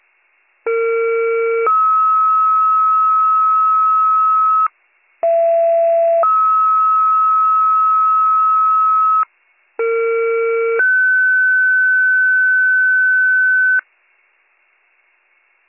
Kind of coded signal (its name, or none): two-tone page